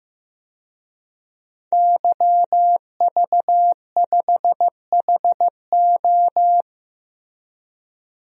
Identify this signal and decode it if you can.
Morse code, heard as a keyed tone, about 16 wpm, 700 Hz. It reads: YV5HO
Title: DXCC-100 Call Signs - 5 Characters - Repeated 2x - 15wpm